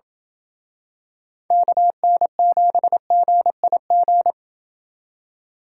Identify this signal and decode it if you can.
Morse code, heard as a keyed tone, about 27 wpm, 700 Hz. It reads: KN7GIG